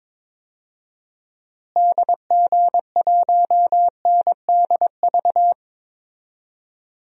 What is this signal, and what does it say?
Morse code, heard as a keyed tone, about 22 wpm, 700 Hz. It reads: DG1NDV
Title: DXCC-100 Call Signs - 6 Characters - Repeated 3x - 22wpm